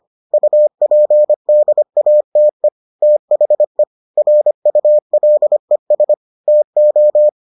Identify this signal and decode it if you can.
Morse code, heard as a keyed tone, about 25 wpm, 600 Hz. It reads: UPDATE THE RULES TO